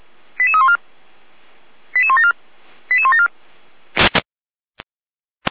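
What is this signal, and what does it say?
Five-tone selective call (ZVEI), heard as quick runs of five tones, tones of about 70 ms. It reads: 88314 88174 88174
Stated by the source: ZVEI five-tone selective call 88156 received by SDR.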